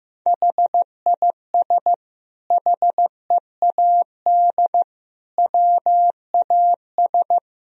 Morse code, heard as a keyed tone, about 15 wpm, 700 Hz. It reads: HIS HEAD WAS
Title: Sentences from Top 1000 Words - Review Part 1 - 15wpm